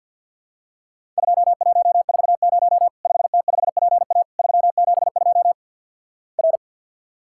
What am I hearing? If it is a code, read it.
Morse code, heard as a keyed tone, about 50 wpm, 700 Hz. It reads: 2140 5T5PA 372 R
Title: Worked All Europe DX Contest - QTC Exchange with Repeats - 50wpm